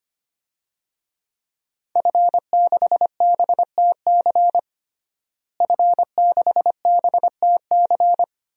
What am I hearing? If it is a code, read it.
Morse code, heard as a keyed tone, about 25 wpm, 700 Hz. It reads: F6BTC F6BTC
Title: DXCC-100 Call Signs - 5 Characters - Repeated 2x - 25wpm